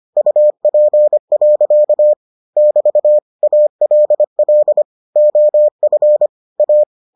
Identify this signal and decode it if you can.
Morse code, heard as a keyed tone, about 25 wpm, 600 Hz. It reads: UP. <BT> ALL OF A